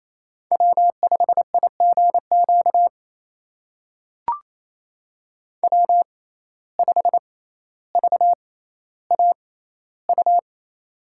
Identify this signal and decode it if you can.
Morse code, heard as a keyed tone, about 28 wpm, spaced as for 14 wpm, 700 Hz. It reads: W5IGQ W5VAU
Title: USA Call Signs - 5 Characters - 6x Character Spacing - 28wpm